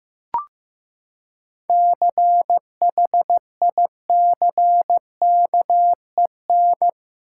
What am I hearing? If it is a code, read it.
Morse code, heard as a keyed tone, about 15 wpm, 700 Hz. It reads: CHICKEN
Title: Seven Letter Words - 15wpm